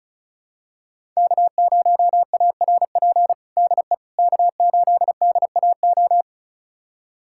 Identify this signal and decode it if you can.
Morse code, heard as a keyed tone, about 36 wpm, 700 Hz. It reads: K0ARP DE K8DAO
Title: USA Callsign DE Callsign - 5 Characters - 35wpm